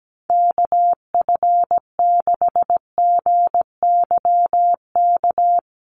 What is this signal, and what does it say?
Morse code, heard as a keyed tone, about 17 wpm, 700 Hz. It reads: KF6GYK
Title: USA Call Signs 17wpm